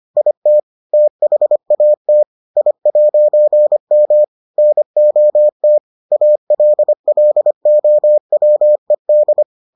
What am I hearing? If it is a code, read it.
Morse code, heard as a keyed tone, about 25 wpm, 600 Hz. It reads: IT THAT I'M NOT ALLOWED